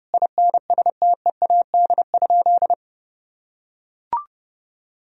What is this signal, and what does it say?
Morse code, heard as a keyed tone, about 28 wpm, 700 Hz. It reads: INSTEAD?
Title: Sentences from Top 1000 Words - Review Part 4 - 30wpm